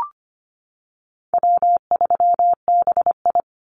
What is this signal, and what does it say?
Morse code, heard as a keyed tone, about 25 wpm, 700 Hz. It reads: W3BI
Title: Call Signs Repeated 2x - 25wpm